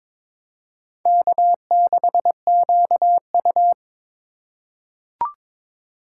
Morse code, heard as a keyed tone, about 22 wpm, 700 Hz. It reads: K6QU
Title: USA Call Signs - 4 Characters - 4x Character Spacing - 22wpm